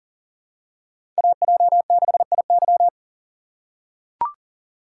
Morse code, heard as a keyed tone, about 40 wpm, 700 Hz. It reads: AJ6IY
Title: USA Call Signs - 5 Characters - 6x Character Spacing - 40wpm